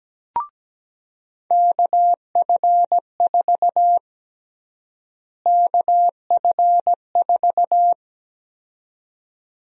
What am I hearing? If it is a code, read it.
Morse code, heard as a keyed tone, about 17 wpm, 700 Hz. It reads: KF4 KF4
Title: USA Call Sign Prefixes - Repeated 2x - 17wpm